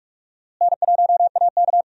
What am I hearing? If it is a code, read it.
Morse code, heard as a keyed tone, about 44 wpm, 700 Hz. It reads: N1AK
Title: USA Call Signs - 4 Characters - Repeated 3x - 45wpm